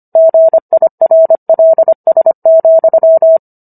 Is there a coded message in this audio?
Yes: Morse code from 0.2 to 3.4 s, heard as a keyed tone, about 25 wpm, 650 Hz. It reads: GIRLS,